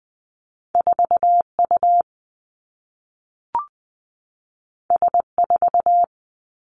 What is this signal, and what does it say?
Morse code, heard as a keyed tone, about 20 wpm, 700 Hz. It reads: 4 U S 4
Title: Two Challenging Letter-Number Combinations 20wpm